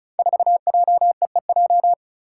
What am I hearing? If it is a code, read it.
Morse code, heard as a keyed tone, about 36 wpm, 700 Hz. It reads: 4JEEJ
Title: Five Challenging Letter-Number Combinations 35wpm